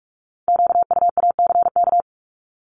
Morse code, heard as a keyed tone, about 46 wpm, 700 Hz. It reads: QUACK
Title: Multi-letter Phonemes Repeated 3x 45wpm